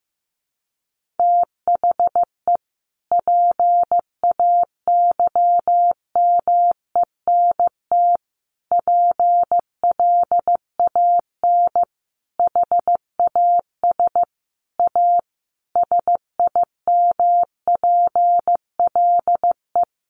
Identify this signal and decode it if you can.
Morse code, heard as a keyed tone, about 15 wpm, 700 Hz. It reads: THE PAYMENT PLAN HAS A SIMPLE
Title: Sentences from Top 1500 Words 15wpm